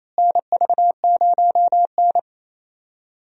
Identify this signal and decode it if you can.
Morse code, heard as a keyed tone, about 28 wpm, 700 Hz. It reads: NV0N
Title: USA Call Signs - 4 Characters - Repeated 3x - 28wpm